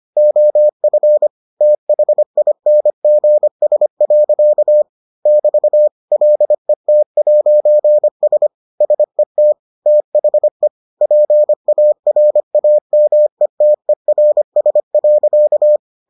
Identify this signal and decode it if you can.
Morse code, heard as a keyed tone, about 25 wpm, 600 Hz. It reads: OF THINGS. <BT> LET'S SET THE PARAMETERS.